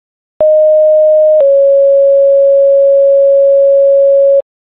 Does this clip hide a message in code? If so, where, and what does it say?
Yes: two-tone page from 0.4 to 4.4 s, heard as a one-second tone, then a three-second tone, about 620 then 550 Hz.